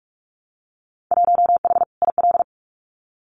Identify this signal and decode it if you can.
Morse code, heard as a keyed tone, about 46 wpm, 700 Hz. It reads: 1H IL
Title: Winter Field Day Exchanges 45wpm